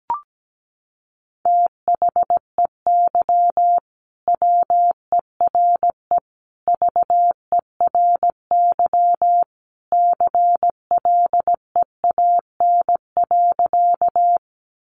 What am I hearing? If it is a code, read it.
Morse code, heard as a keyed tone, about 17 wpm, 700 Hz. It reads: THEY WERE VERY CLEAN.
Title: Sentences from Top 1100 Words 17wpm